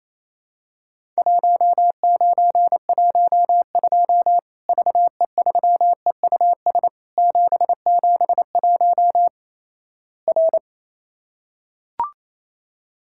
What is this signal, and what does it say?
Morse code, heard as a keyed tone, about 28 wpm, 700 Hz. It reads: 1912 VE3EUS 771 R
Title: Worked All Europe DX Contest - QTC Exchange - 28wpm